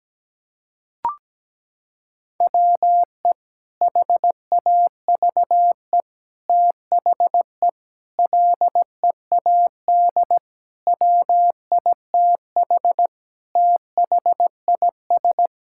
Morse code, heard as a keyed tone, about 17 wpm, 700 Hz. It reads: WE HAVE THE LEAD WITH THIS
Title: Sentences from Top 1000 Words - Review Part 5 - 17wpm